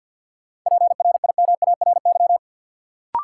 Morse code, heard as a keyed tone, about 51 wpm, 700 Hz. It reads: PRIMARY
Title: Sentences from Top 1000 Words - Review Part 5 - New Word Drill - 50wpm